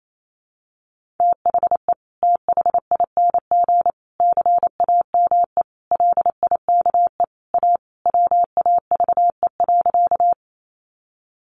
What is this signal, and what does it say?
Morse code, heard as a keyed tone, about 28 wpm, 700 Hz. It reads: THE THING CAME LIKE A WAVE.
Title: Sentences from Top 1400 Words 28wpm